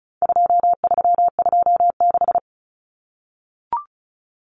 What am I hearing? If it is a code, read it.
Morse code, heard as a keyed tone, about 34 wpm, 700 Hz. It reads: 2326
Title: Common Cut Numbers - 4-Digit 35wpm